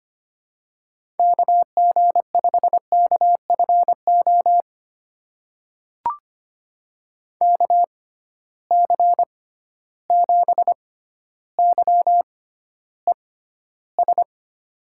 Morse code, heard as a keyed tone, about 25 wpm, spaced as for 13 wpm, 700 Hz. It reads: KG5KFO KC7YES
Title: USA Call Signs - 6 Characters - 6x Character Spacing - 25wpm